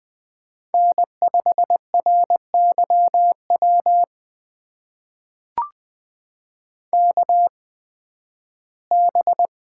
Morse code, heard as a keyed tone, about 20 wpm, spaced as for 13 wpm, 700 Hz. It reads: N5RYW KB
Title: USA Call Signs - 5 Characters - 8x Character Spacing - 20wpm